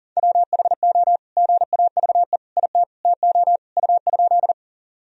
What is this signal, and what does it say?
Morse code, heard as a keyed tone, about 40 wpm, 700 Hz. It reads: WHO GAVE IT TO U?
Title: Sentences from Top 500 Words and Abbreviations 40wpm